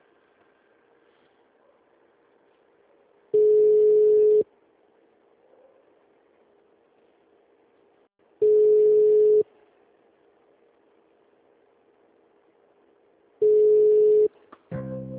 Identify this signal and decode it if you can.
Ringback tone (Europe, 425 Hz).
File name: out- -2014- - - .15